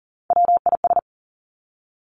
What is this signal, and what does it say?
Morse code, heard as a keyed tone, about 40 wpm, 700 Hz. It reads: WIS
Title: Three Challenging Letter-Number Combinations 40wpm